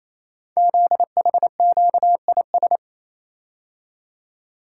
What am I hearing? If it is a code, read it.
Morse code, heard as a keyed tone, about 27 wpm, 700 Hz. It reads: ZHQIS
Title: Code Groups - 5 Letters - 28wpm